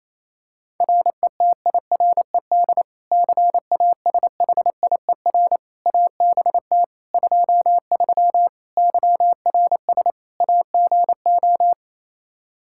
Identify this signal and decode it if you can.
Morse code, heard as a keyed tone, about 28 wpm, 700 Hz. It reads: RETIRED CASHIER ABT 23 YRS AGO
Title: Practice QSO - Exchange 3 - More Info - 28wpm